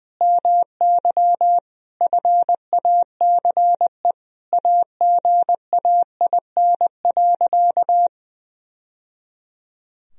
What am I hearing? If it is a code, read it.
Morse code, heard as a keyed tone, about 20 wpm, 700 Hz. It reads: MY FACE AGAIN.
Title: Sentences from Top 1000 Words - Review Part 2 - 20wpm